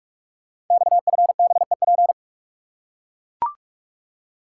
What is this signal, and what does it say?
Morse code, heard as a keyed tone, about 44 wpm, 700 Hz. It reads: XFBEP
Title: Code Groups - 5 Letter:Numbers - 45wpm